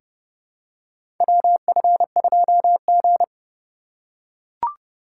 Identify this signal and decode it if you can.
Morse code, heard as a keyed tone, about 32 wpm, 700 Hz. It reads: WF2G
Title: USA Call Signs - 4 Characters - 3x Character Spacing - 30wpm